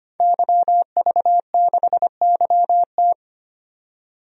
Morse code, heard as a keyed tone, about 25 wpm, 700 Hz. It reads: YV6YT
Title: DXCC-100 Call Signs - 5 Characters - 25wpm